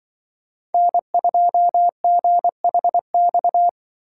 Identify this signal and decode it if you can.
Morse code, heard as a keyed tone, about 24 wpm, 700 Hz. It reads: N2GHX
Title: Speed Racing - USA Call Signs 20wpm